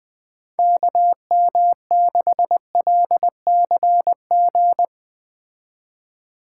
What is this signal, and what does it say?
Morse code, heard as a keyed tone, about 21 wpm, 700 Hz. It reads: KM6LCG
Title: Speed Racing - USA 2x3 Call Signs 17wpm